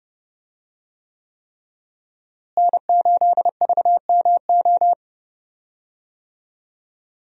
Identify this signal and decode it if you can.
Morse code, heard as a keyed tone, about 32 wpm, 700 Hz. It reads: N8VMO
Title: USA Call Signs - 5 Characters - Repeated 3x - 30wpm